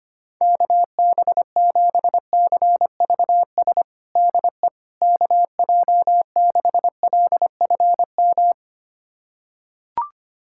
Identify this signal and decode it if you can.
Morse code, heard as a keyed tone, about 25 wpm, 700 Hz. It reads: KB7CVS DE KJ6LFM
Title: USA Callsign DE Callsign - 6 Characters - 25wpm